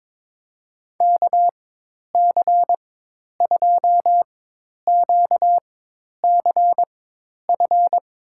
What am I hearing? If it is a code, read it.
Morse code, heard as a keyed tone, about 22 wpm, spaced as for 10 wpm, 700 Hz. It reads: KC2QCF
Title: USA Call Signs - 6 Characters - 4x Character Spacing - 22wpm